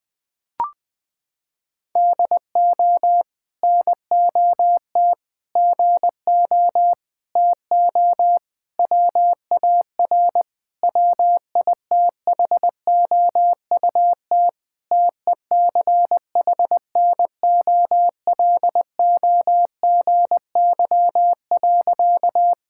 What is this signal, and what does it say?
Morse code, heard as a keyed tone, about 20 wpm, 700 Hz. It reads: DO NOT GO TO WAR WITHOUT TECHNOLOGY.
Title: Sentences from Top 1000 Words - Review Part 5 - 20wpm